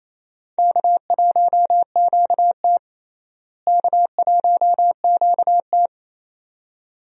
Morse code, heard as a keyed tone, about 27 wpm, 700 Hz. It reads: K1QT K1QT
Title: USA Call Signs - 4 Characters - Repeated 2x - 28wpm